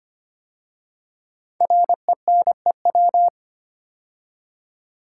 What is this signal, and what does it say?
Morse code, heard as a keyed tone, about 25 wpm, 700 Hz. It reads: RENEW
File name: 5 Letter Words - First 3 Letters Given - 25wpm